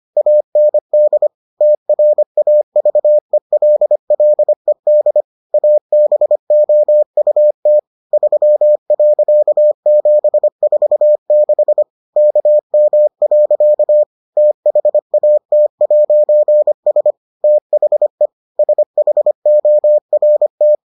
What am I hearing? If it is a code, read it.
Morse code, heard as a keyed tone, about 25 wpm, 600 Hz. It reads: AND TRAVELLED ABOUT 3.746 KM. THAT'S THE SHORT